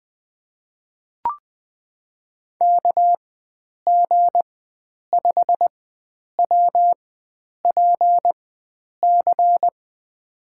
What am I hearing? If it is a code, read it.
Morse code, heard as a keyed tone, about 20 wpm, spaced as for 9 wpm, 700 Hz. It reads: KG5WPC
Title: USA Call Signs - 6 Characters - 4x Character Spacing - 20wpm